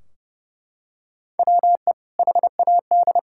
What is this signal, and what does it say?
Morse code, heard as a keyed tone, about 32 wpm, 700 Hz. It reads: WE HAD
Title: Sentences from Top 1600 Words 30wpm